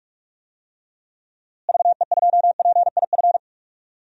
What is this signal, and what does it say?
Morse code, heard as a keyed tone, about 46 wpm, 700 Hz. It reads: VE2PIF
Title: DXCC-100 Call Signs - 6 Characters - Repeated 2x - 45wpm